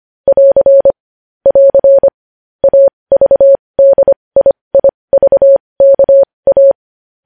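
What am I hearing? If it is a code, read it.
Morse code, heard as a keyed tone, about 25 wpm, spaced as for 20 wpm, 550 Hz. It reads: <AR> <AR> AVDIIVKA
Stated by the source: Morse news headlines practice at 20 wpm.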